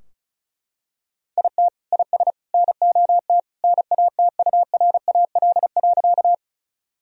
Morse code, heard as a keyed tone, about 34 wpm, 700 Hz. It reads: IT IS NOT NATURAL.
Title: Sentences from Top 1000 Words - Review Part 4 - 35wpm